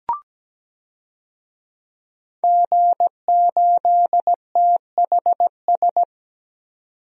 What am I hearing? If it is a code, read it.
Morse code, heard as a keyed tone, about 17 wpm, 700 Hz. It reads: G8THS
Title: DXCC-100 Call Signs - 5 Characters - 17wpm